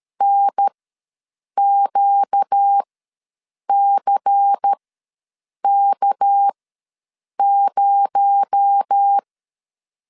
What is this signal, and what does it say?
Morse code, heard as a keyed tone, about 13 wpm, spaced as for 7 wpm, 800 Hz. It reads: NQCK0
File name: LEZIONI / 6 PKX0 / 6 PKX0 ws 12wpm 3448